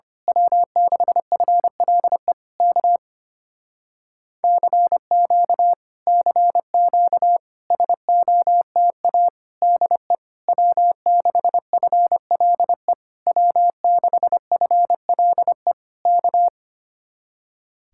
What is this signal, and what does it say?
Morse code, heard as a keyed tone, about 24 wpm, 700 Hz. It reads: W6FLE K CQ CQ SOTA DE W6FLE W6FLE K
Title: Speed-Racing Common QSO Phrases - 25wpm